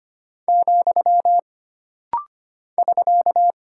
Morse code, heard as a keyed tone, about 25 wpm, 700 Hz. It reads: , <SK>
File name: Single Punctuation-Prosign - Rapid-Fire - 25wpm